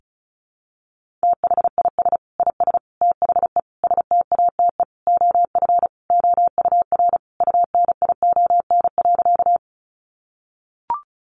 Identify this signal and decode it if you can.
Morse code, heard as a keyed tone, about 34 wpm, 700 Hz. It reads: THIS IS THE STATE OF OUR UNION.
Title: Sentences from Top 1400 Words 35wpm